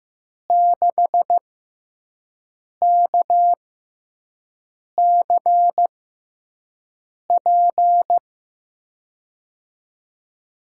Morse code, heard as a keyed tone, about 15 wpm, spaced as for 5 wpm, 700 Hz. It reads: 6KCP
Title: USA Call Signs - 6 Characters - 6x Character Spacing - 15wpm